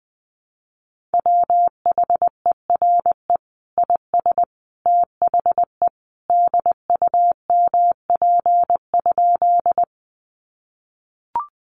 Morse code, heard as a keyed tone, about 19 wpm, 700 Hz. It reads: WHERE IS THE DUMP?
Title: Sentences from Top 1300 Words 20wpm